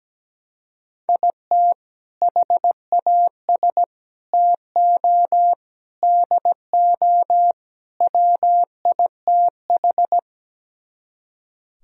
Morse code, heard as a keyed tone, about 17 wpm, 700 Hz. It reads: IT HAS TO DO WITH
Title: Speed-Racing - Sets of 5 Words 17wpm